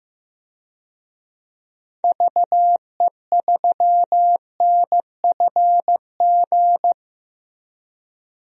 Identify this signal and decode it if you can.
Morse code, heard as a keyed tone, about 15 wpm, 700 Hz. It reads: VE3NFG
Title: DXCC-100 Call Signs - 6 Characters - Repeated 3x - 15wpm